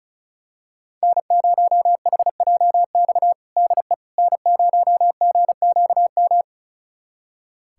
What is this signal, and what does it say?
Morse code, heard as a keyed tone, about 34 wpm, 700 Hz. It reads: N0HJX DE N0GQM
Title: USA Callsign DE Callsign - 5 Characters - 35wpm